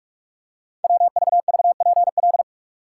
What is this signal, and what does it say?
Morse code, heard as a keyed tone, about 44 wpm, 700 Hz. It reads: WVVPL
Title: Code Groups - 5 Letter:Numbers - 45wpm